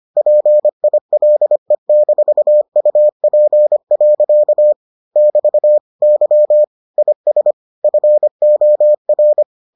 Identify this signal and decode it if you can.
Morse code, heard as a keyed tone, about 25 wpm, 600 Hz. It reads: PILE-UP. <BT> Y IS FOR